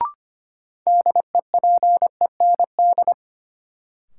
Morse code, heard as a keyed tone, about 25 wpm, 700 Hz. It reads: DEPEND
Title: Sentences from Top 1300 Words - New Word Drill - 25wpm